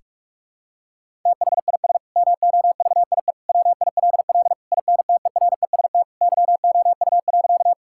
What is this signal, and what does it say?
Morse code, heard as a keyed tone, about 41 wpm, 700 Hz. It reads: THIS MOVIE WILL INTEREST YOU.
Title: Sentences from Top 1000 Words - Review Part 5 - 45wpm